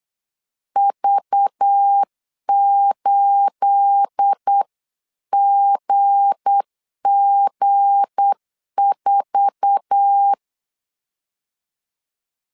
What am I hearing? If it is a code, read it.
Morse code, heard as a keyed tone, about 8 wpm, 800 Hz. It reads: V8GG4
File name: CW / DET / 1 9 1500C S A 8wpm 5637